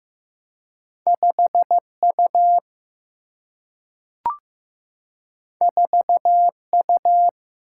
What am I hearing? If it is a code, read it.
Morse code, heard as a keyed tone, about 15 wpm, 700 Hz. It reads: 5 U 4 U
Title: Single-Letter Single-Number 15wpm